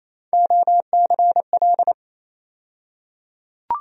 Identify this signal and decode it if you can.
Morse code, heard as a keyed tone, about 28 wpm, 700 Hz. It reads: OCL
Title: USA Call Sign Postfixes - 3 Letters 28wpm